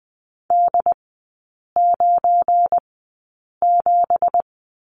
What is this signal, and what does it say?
Morse code, heard as a keyed tone, about 20 wpm, spaced as for 8 wpm, 700 Hz. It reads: D97
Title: 178 Lesson 29e - Slash - Three Characters 20wpm